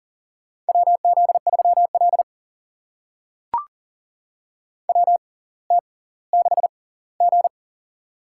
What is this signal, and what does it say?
Morse code, heard as a keyed tone, about 40 wpm, spaced as for 20 wpm, 700 Hz. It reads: WZ3L WT6G
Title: USA Call Signs - 4 Characters - 6x Character Spacing - 40wpm